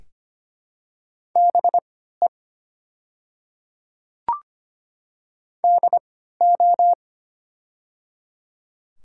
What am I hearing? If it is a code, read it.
Morse code, heard as a keyed tone, about 25 wpm, spaced as for 14 wpm, 700 Hz. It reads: BE DO
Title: 3x Character Spacing - Top 200 Words - 25wpm